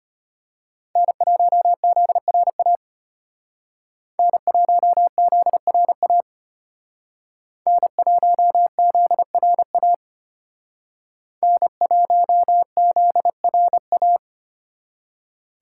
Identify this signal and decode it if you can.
Morse code, heard as a keyed tone, about 32 wpm, 700 Hz. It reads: N1ZRA N1ZRA N1ZRA N1ZRA